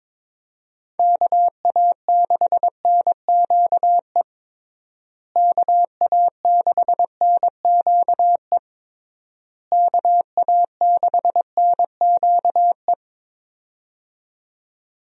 Morse code, heard as a keyed tone, about 22 wpm, 700 Hz. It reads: KA6NQE KA6NQE KA6NQE
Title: USA Call Signs - 6 Characters - Repeated 3x - 22wpm